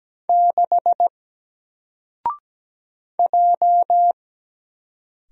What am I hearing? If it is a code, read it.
Morse code, heard as a keyed tone, about 17 wpm, 700 Hz. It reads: 6 J
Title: Single Letter-Number 17wpm